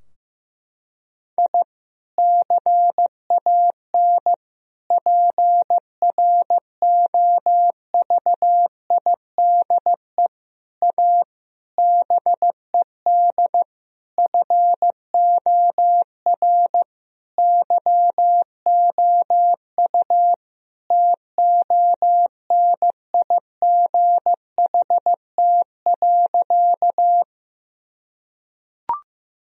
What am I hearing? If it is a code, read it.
Morse code, heard as a keyed tone, about 16 wpm, 700 Hz. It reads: I CAN PROVIDE A BED FOR YOU TONIGHT.
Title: Sentences from Top 1000 Words - Review Part 3 - 15wpm